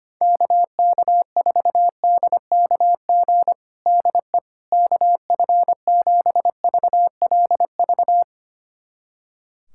Morse code, heard as a keyed tone, about 25 wpm, 700 Hz. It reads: KK4DKG DE KF7VLV
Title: USA Callsign DE Callsign 25wpm